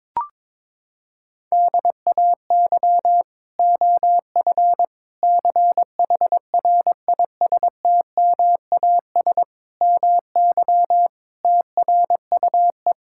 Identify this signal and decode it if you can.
Morse code, heard as a keyed tone, about 22 wpm, 700 Hz. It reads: DAY OF CHRISTMAS MY TRUE
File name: Sets of 9 Words - 22wpm